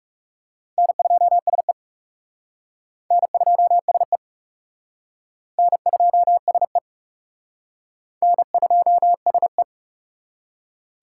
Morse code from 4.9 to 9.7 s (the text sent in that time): N2SE N2SE